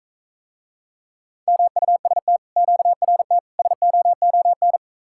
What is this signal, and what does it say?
Morse code, heard as a keyed tone, about 45 wpm, 700 Hz. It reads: MUST QRT SOON